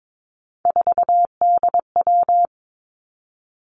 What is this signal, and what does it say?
Morse code, heard as a keyed tone, about 22 wpm, 700 Hz. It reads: 4DW
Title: Three Challenging Letter-Number Combinations 22wpm